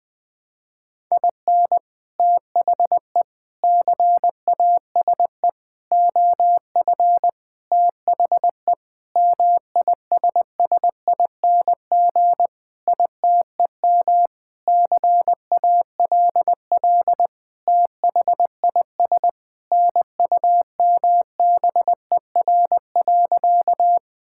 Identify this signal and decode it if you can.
Morse code, heard as a keyed tone, about 20 wpm, 700 Hz. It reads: IN THE CASE OF THE MISSING ITEM CALL THIS NUMBER.
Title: Sentences with 5-Word Phrases - 20wpm